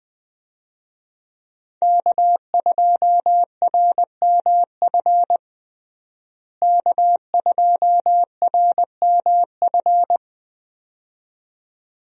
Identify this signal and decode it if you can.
Morse code, heard as a keyed tone, about 20 wpm, 700 Hz. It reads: K2RMF K2RMF
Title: USA Call Signs - 5 Characters - Repeated 2x - 20wpm